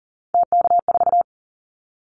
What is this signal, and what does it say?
Morse code, heard as a keyed tone, about 38 wpm, 700 Hz. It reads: TK4
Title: Callsign Prefixes for Top 100 DXCC - 40wpm